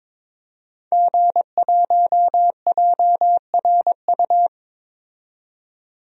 Morse code, heard as a keyed tone, about 22 wpm, 700 Hz. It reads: G1JRU
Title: DXCC-100 Call Signs - 5 Characters - 22wpm